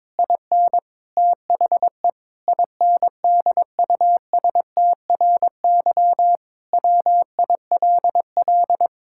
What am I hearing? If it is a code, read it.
Morse code, heard as a keyed tone, about 22 wpm, 700 Hz. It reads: IN THE INDUSTRY WILL
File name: Sentences with 7-Word Phrases - 22wpm